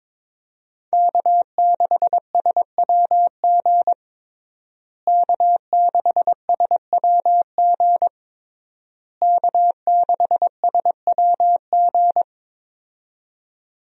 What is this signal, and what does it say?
Morse code, heard as a keyed tone, about 22 wpm, 700 Hz. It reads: K6SWG K6SWG K6SWG
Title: USA Call Signs - 5 Characters - Repeated 3x - 22wpm